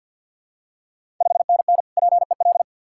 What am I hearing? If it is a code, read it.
Morse code, heard as a keyed tone, about 52 wpm, 690 Hz. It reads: 5NN PEL
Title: Ontario QSO Party Exchange 50wpm